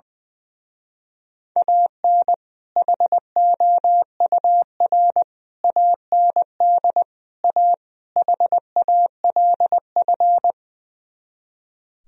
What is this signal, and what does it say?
Morse code, heard as a keyed tone, about 21 wpm, 700 Hz. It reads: AN HOUR AND A HALF
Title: Speed-Racing - Sets of 5 Words 20wpm